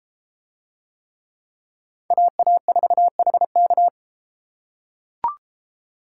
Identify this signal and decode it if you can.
Morse code, heard as a keyed tone, about 33 wpm, 700 Hz. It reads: AA4HK